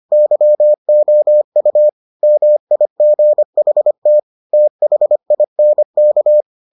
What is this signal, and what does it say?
Morse code, heard as a keyed tone, about 25 wpm, 600 Hz. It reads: YOU MIGHT THINK